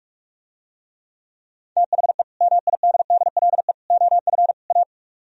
Morse code, heard as a keyed tone, about 44 wpm, 700 Hz. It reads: THE MIDDLE OF A